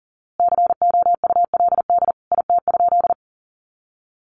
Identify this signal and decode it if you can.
Morse code, heard as a keyed tone, about 37 wpm, 700 Hz. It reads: COULD IT?
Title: A Sound Of Thunder 40wpm